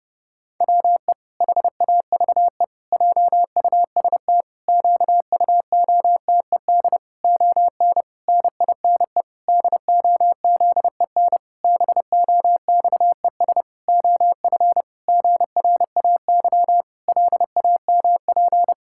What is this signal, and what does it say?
Morse code, heard as a keyed tone, about 32 wpm, 700 Hz. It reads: WE HAVE JUST QUOTED ON NINE DOZEN BOXES OF GRAY LAMP
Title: Pangram Challenge v2 - 30wpm